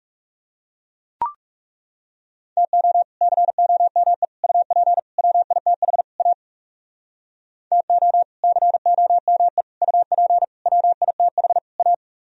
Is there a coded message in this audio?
Yes: Morse code from 2.6 to 11.9 s, heard as a keyed tone, about 40 wpm, 700 Hz. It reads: TO COME UP WITH A TO COME UP WITH A